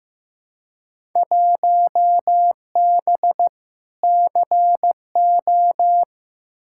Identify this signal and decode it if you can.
Morse code, heard as a keyed tone, about 15 wpm, 700 Hz. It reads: 1B CO
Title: Field Day Exchanges - 15wpm